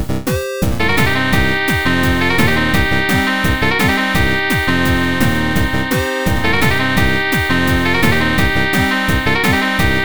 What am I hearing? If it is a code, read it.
There is no code.